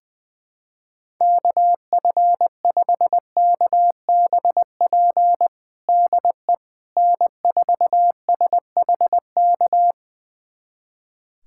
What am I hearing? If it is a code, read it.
Morse code, heard as a keyed tone, about 21 wpm, 700 Hz. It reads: KF5KBP DE N4SHK
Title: USA Callsign DE Callsign 20wpm